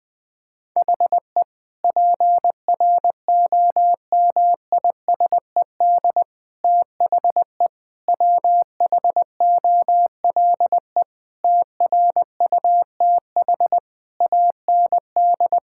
Morse code, heard as a keyed tone, about 20 wpm, 700 Hz. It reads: HE PROMISED THE WHOLE TRUTH AND
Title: Sentences with 8-Word Phrases - 20wpm